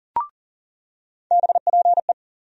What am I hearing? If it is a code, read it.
Morse code, heard as a keyed tone, about 40 wpm, 700 Hz. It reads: BPE
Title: USA Call Sign Postfixes - 3 Letters 40wpm